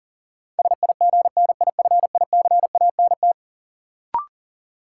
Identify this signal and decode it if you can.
Morse code, heard as a keyed tone, about 40 wpm, 700 Hz. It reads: SIGNIFICANT
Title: Sentences from Top 1000 Words - Review Part 6 - New Word Drill - 40wpm